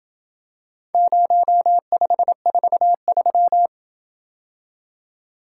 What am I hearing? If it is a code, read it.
Morse code, heard as a keyed tone, about 28 wpm, 700 Hz. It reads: 0543